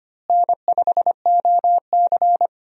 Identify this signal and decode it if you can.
Morse code, heard as a keyed tone, about 25 wpm, 700 Hz. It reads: N5OC
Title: USA Call Signs - 4 Characters - Repeated 2x - 25wpm